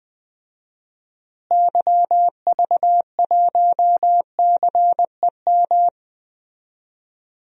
Morse code, heard as a keyed tone, about 21 wpm, 700 Hz. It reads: YV1CEM
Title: DXCC-100 Call Signs - 6 Characters - Repeated 3x - 20wpm